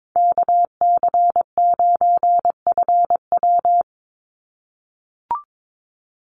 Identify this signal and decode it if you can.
Morse code, heard as a keyed tone, about 22 wpm, 700 Hz. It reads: KC9FW
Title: USA Call Signs 22wpm